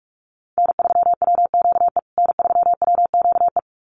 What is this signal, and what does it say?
Morse code, heard as a keyed tone, about 44 wpm, 700 Hz. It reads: N3WQE N3WQE
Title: USA Call Signs Repeated 2x 45wpm